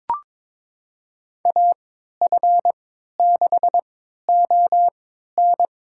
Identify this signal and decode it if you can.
Morse code, heard as a keyed tone, about 22 wpm, spaced as for 12 wpm, 700 Hz. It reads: AF6ON
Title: USA Call Signs - 5 Characters - 3x Character Spacing - 22wpm